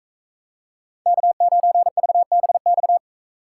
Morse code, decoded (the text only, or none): K9VBX